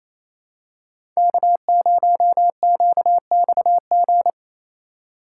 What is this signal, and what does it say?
Morse code, heard as a keyed tone, about 28 wpm, 700 Hz. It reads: K0QXG
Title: USA Call Signs - 5 Characters - 4x Character Spacing - 28wpm